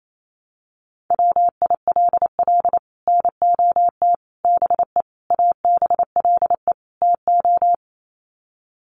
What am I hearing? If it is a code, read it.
Morse code, heard as a keyed tone, about 28 wpm, 700 Hz. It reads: WILL NOT BE ABLE TO
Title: Sets of 5 Words 28wpm